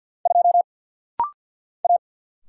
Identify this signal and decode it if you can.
Morse code, heard as a keyed tone, about 50 wpm, 700 Hz. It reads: 2 A
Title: Single Letter-Number Alt - Rapid-Fire 50wpm